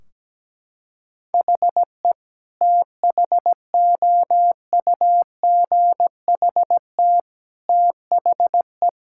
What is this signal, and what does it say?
Morse code, heard as a keyed tone, about 17 wpm, 700 Hz. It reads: HE THOUGHT THE
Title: Sentences from Top 1600 Words 17wpm